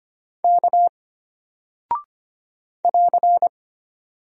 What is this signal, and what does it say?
Morse code, heard as a keyed tone, about 25 wpm, 700 Hz. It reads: K <AR>
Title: Single Character (Letter:Number:Punctuation:Prosign) Alt 25wpm